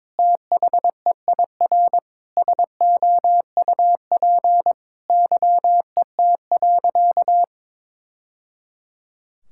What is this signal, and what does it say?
Morse code, heard as a keyed tone, about 22 wpm, 700 Hz. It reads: THEIR SOUP YET.
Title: Sentences from Top 1000 Words - Review Part 6 - 22wpm